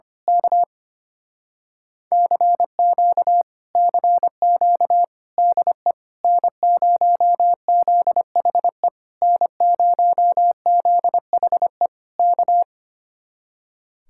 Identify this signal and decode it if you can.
Morse code, heard as a keyed tone, about 25 wpm, 700 Hz. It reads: K CQ CQ DE N0ZHE N0ZHE K
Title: Speed-Racing Common QSO Phrases - 25wpm